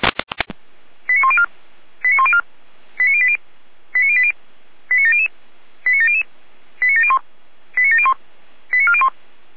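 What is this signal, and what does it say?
Five-tone selective call (ZVEI), heard as quick runs of five tones, tones of about 80 ms. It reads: 89104 89104 89080 89080 89700 89700 89701 89701 89401
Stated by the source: ZVEI five-tone selective call 89092 received on SDR.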